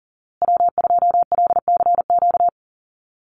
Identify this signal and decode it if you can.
Morse code, heard as a keyed tone, about 40 wpm, 700 Hz. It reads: W2LCQ
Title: Call Signs Repeated 2x - 40wpm